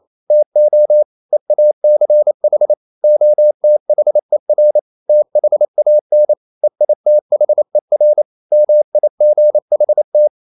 Morse code, heard as a keyed tone, about 28 wpm, 600 Hz. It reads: TO EACH OTHER THAN EITHER MIGHT
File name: CW / Practice / EnglishQuotations0128